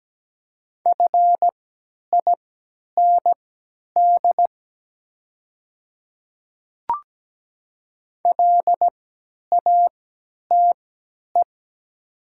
Morse code, heard as a keyed tone, about 17 wpm, spaced as for 10 wpm, 700 Hz. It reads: FIND LATE
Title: 3x Character Spacing - Top 300 Words - 17wpm